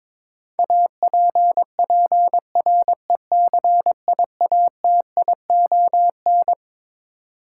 Morse code, heard as a keyed tone, about 22 wpm, 700 Hz. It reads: APPRECIATION
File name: Twelve Letter Words - 22wpm